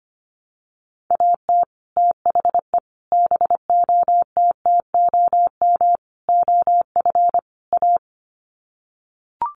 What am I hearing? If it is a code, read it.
Morse code, heard as a keyed tone, about 25 wpm, 700 Hz. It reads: AT THE BOTTOM OF A
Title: Sets of 5 Words Encore2 25wpm